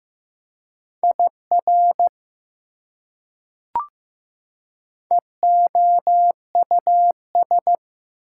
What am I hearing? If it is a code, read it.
Morse code, heard as a keyed tone, about 15 wpm, 700 Hz. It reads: IR EOUS
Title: Prefixes and Suffixes 15wpm